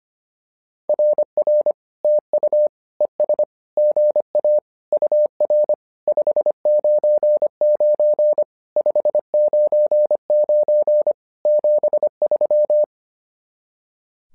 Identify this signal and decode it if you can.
Morse code, heard as a keyed tone, about 25 wpm, 600 Hz. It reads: RR TU ES GA UR 599 599 73
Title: SOTA QSO Practice 25wpm